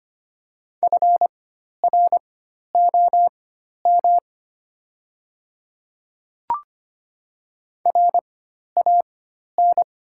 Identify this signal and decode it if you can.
Morse code, heard as a keyed tone, about 25 wpm, spaced as for 12 wpm, 690 Hz. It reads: FROM RAN
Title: 4x Character Spacing - Top 500 Words - 25wpm